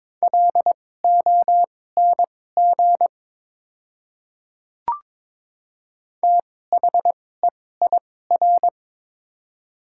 Morse code, heard as a keyed tone, about 22 wpm, spaced as for 16 wpm, 700 Hz. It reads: LONG THEIR
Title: 2x Character Spacing - Top 100 Words - 22wpm